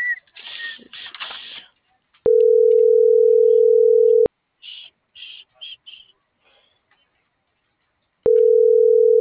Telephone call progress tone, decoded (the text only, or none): ringback tone